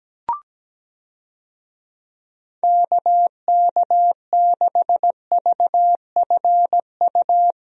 Morse code, heard as a keyed tone, about 17 wpm, 700 Hz. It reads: KK6VFU
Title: USA Call Signs - 6 Characters - Repeated 3x - 17wpm